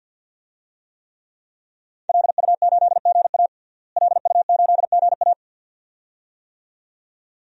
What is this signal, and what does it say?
Morse code, heard as a keyed tone, about 49 wpm, 700 Hz. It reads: LU8GA LU8GA
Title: DXCC-100 Call Signs - 5 Characters - Repeated 2x - 50wpm